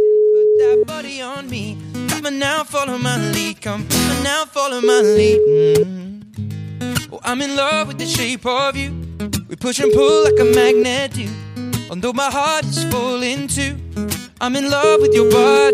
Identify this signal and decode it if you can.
Ringback tone (Europe, 425 Hz), with speech.